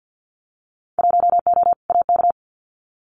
Morse code, heard as a keyed tone, about 50 wpm, 700 Hz. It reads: 1O AK
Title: Winter Field Day Exchanges Encore 50wpm